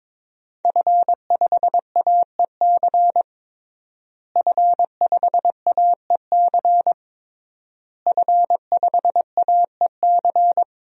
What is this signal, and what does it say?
Morse code, heard as a keyed tone, about 23 wpm, 700 Hz. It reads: F5AEC F5AEC F5AEC
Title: DXCC-100 Call Signs - 5 Characters - Repeated 3x - 22wpm